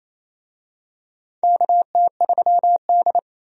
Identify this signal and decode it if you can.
Morse code, heard as a keyed tone, about 27 wpm, 700 Hz. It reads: KT3D
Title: USA Call Signs - 4 Characters - 28wpm